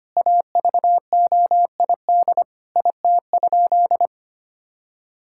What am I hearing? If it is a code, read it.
Morse code, heard as a keyed tone, about 25 wpm, 700 Hz. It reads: AVOID IT?
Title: Sentences from Top 1000 Words - Review Part 6 - 25wpm